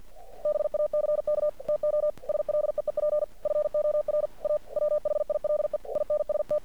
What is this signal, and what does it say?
Morse code, heard as a keyed tone, about 50 wpm, 600 Hz. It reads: BACK TO SLEEP FOR A WHILE INST
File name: archived / CRSnet